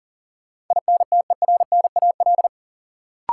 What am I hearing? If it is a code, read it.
Morse code, heard as a keyed tone, about 40 wpm, 700 Hz. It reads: INTERNAL
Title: Sentences from Top 1400 Words - New Word Drill - 40wpm